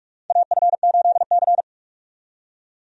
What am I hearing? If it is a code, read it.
Morse code, heard as a keyed tone, about 42 wpm, 700 Hz. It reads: AF8C
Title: USA Call Signs - 4 Characters - 45wpm